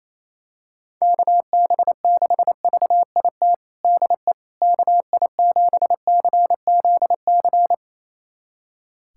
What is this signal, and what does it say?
Morse code, heard as a keyed tone, about 28 wpm, 700 Hz. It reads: KB6VIT DE KI7CZC
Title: USA Callsign DE Callsign - 6 Characters - 28wpm